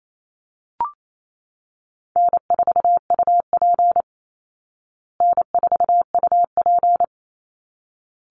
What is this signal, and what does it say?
Morse code, heard as a keyed tone, about 28 wpm, 690 Hz. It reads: N4UP N4UP
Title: Call Signs Repeated 3x - 28wpm